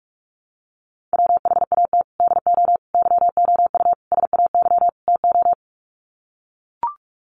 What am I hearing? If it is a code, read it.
Morse code, heard as a keyed tone, about 44 wpm, 700 Hz. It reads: WHAT DO YOU SAY TO
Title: Sets of 5 Words 45wpm